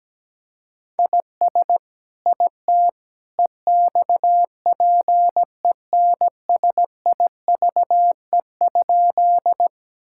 Morse code, heard as a keyed tone, about 17 wpm, 700 Hz. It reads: IS IT EXPENSIVE?
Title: Sentences from Top 1000 Words - Review Part 6 - 17wpm